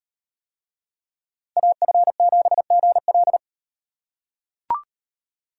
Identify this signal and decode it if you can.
Morse code, heard as a keyed tone, about 38 wpm, 700 Hz. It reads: AF7GL